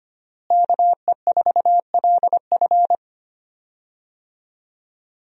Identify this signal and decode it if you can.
Morse code, heard as a keyed tone, about 25 wpm, 700 Hz. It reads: KE4LF
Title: USA Call Signs - 5 Characters - Repeated 2x - 25wpm